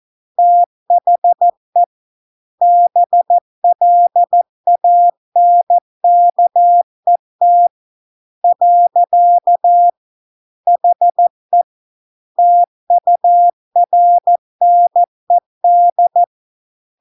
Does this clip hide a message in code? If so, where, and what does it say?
Yes: Morse code from 0.4 to 16.2 s, heard as a keyed tone, about 14 wpm, 700 Hz. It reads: THE BLANKET . HE TURNED